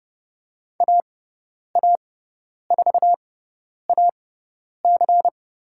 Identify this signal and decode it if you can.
Morse code, heard as a keyed tone, about 30 wpm, spaced as for 10 wpm, 700 Hz. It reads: AA4AC
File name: Lesson 16g - Letter c - Call Signs 30wpm